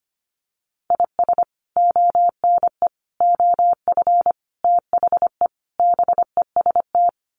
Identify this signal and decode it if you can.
Morse code, heard as a keyed tone, about 25 wpm, 700 Hz. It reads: IS ONE OF THE BEST
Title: Sets of 5 Words 25wpm